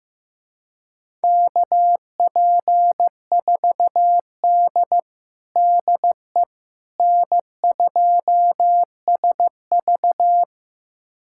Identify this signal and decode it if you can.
Morse code, heard as a keyed tone, about 15 wpm, 700 Hz. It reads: KP4D DE N2SV
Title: USA Callsign DE Callsign - 4 Characters - 15wpm